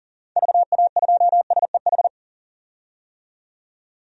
Morse code, heard as a keyed tone, about 39 wpm, 700 Hz. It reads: VA2SEH
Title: DXCC-100 Call Signs - 6 Characters - 40wpm